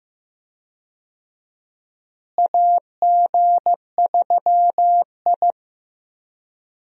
Morse code, heard as a keyed tone, about 16 wpm, 700 Hz. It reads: AG3I
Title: USA Call Signs - 4 Characters - Repeated 2x - 15wpm